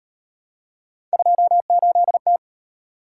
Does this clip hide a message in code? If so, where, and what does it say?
Yes: Morse code from 1.1 to 2.4 s, heard as a keyed tone, about 33 wpm, 700 Hz. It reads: 28T